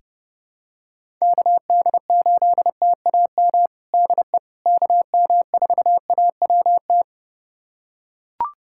Morse code, heard as a keyed tone, about 31 wpm, 700 Hz. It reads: KD8TAM DE KM4AWT
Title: USA Callsign DE Callsign - 6 Characters - 30wpm